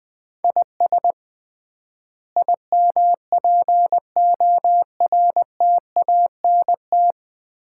Morse code, heard as a keyed tone, about 20 wpm, 700 Hz. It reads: IS IMPORTANT
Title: 3x Word-Spacing - Sets of 5 Words - 20wpm